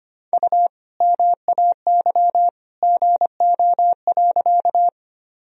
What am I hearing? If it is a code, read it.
Morse code, heard as a keyed tone, about 25 wpm, 700 Hz. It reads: U MAY GO.
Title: Sentences from Top 100 Words and Abbreviations 25wpm